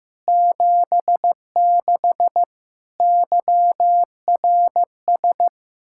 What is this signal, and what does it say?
Morse code, heard as a keyed tone, about 15 wpm, 700 Hz. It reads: 76 YRS
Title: Practice QSO - Exchange 3 - More Info - 15wpm